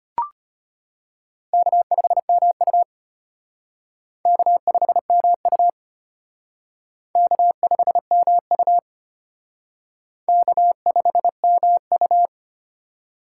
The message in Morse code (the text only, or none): K5MU K5MU K5MU K5MU